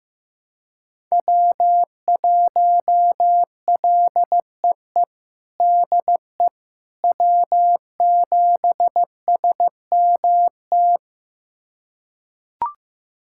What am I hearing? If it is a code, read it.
Morse code, heard as a keyed tone, about 15 wpm, 700 Hz. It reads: W1LEE DE W7SMT
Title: USA Callsign DE Callsign - 5 Characters - 15wpm